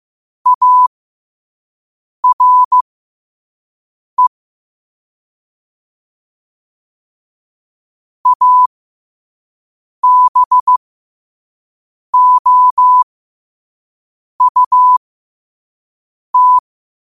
Morse code, heard as a keyed tone, about 15 wpm, spaced as for 5 wpm, 1000 Hz. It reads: ARE ABOUT
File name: qotdfiles / 5wpm